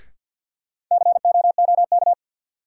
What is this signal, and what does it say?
Morse code, heard as a keyed tone, about 52 wpm, 700 Hz. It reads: COOK
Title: Sentences from Top 700 Words - New Word Drill - 50wpm